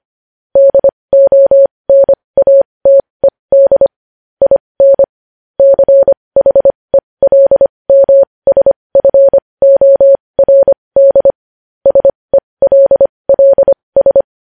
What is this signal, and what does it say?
Morse code, heard as a keyed tone, about 25 wpm, spaced as for 20 wpm, 550 Hz. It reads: DONATED IN CHELMSFORD SELLS
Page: news-headlines-in-morse-code-at-20-wpm